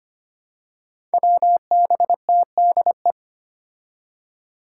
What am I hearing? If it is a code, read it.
Morse code, heard as a keyed tone, about 25 wpm, 700 Hz. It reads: WBTDE
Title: Five Challenging Letter-Number Combinations 25wpm